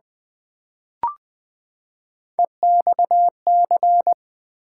Morse code, heard as a keyed tone, about 20 wpm, 700 Hz. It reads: EXC
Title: Build A Word - 3 Letters At A Time - 9-Letter Words - 20wpm